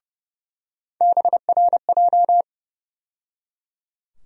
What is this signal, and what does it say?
Morse code, heard as a keyed tone, about 30 wpm, 700 Hz. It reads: BRJ
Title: USA Call Sign Postfixes - 3 Letters 30wpm